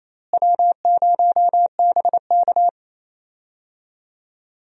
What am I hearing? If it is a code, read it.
Morse code, heard as a keyed tone, about 28 wpm, 700 Hz. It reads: W0BK
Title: USA Call Signs - 4 Characters - Repeated 2x - 28wpm